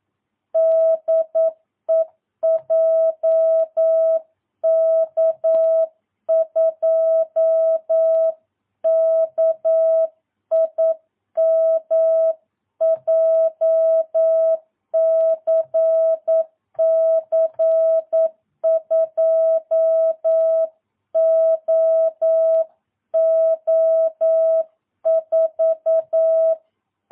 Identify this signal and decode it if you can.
Morse code, heard as a keyed tone, about 9 wpm, 650 Hz. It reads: DE JK2KIM JCC 2OO4
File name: jh8cbh / 2022cw